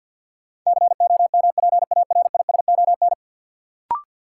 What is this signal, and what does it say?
Morse code, heard as a keyed tone, about 53 wpm, 700 Hz. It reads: COMPARISON
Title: Sentences from Top 1600 Words - New Word Drill Alt - 50wpm